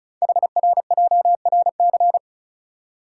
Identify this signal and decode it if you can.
Morse code, heard as a keyed tone, about 36 wpm, 690 Hz. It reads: HRJRC
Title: Code Groups - 5 Letters - 35wpm